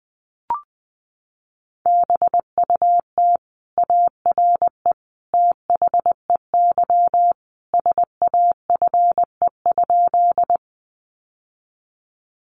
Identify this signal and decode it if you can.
Morse code, heard as a keyed tone, about 20 wpm, 700 Hz. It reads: BUT ARE THEY SAFE?
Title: Sentences from Top 900 Words 20wpm